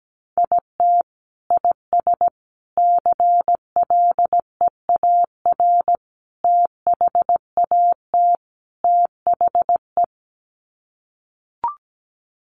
Morse code, heard as a keyed tone, about 17 wpm, 690 Hz. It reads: IT IS CLEAR THAT THE
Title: Sets of 5 Words Encore 17wpm